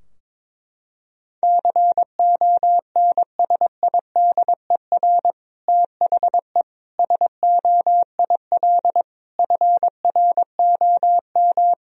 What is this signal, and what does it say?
Morse code, heard as a keyed tone, about 21 wpm, 700 Hz. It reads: CONSIDER THE SOIL FROM
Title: Sentences from Top 1000 Words - Review Part 4 - 22wpm